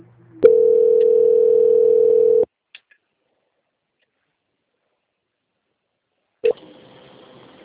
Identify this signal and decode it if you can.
Ringback tone (North America), 440 + 480 Hz.
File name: out- -2052- - - .409